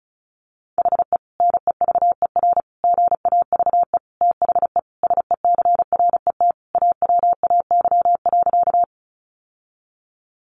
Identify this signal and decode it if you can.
Morse code, heard as a keyed tone, about 34 wpm, 700 Hz. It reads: HE NEVER GAVE THE SECRET AWAY.
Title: Sentences from Top 1200 Words 35wpm